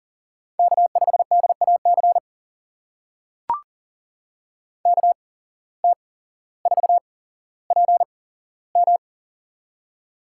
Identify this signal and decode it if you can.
Morse code, heard as a keyed tone, about 39 wpm, spaced as for 17 wpm, 700 Hz. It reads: K5DAC KT4PM
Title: USA Call Signs - 5 Characters - 8x Character Spacing - 40wpm